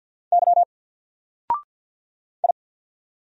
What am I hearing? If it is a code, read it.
Morse code, heard as a keyed tone, about 50 wpm, 700 Hz. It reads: Y I
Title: Single Character (Letter:Number:Punctuation:Prosign) Alt 50wpm